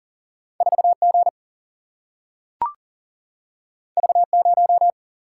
Morse code, heard as a keyed tone, about 39 wpm, 700 Hz. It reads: 4G V0
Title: Single-Letter Single-Number 40wpm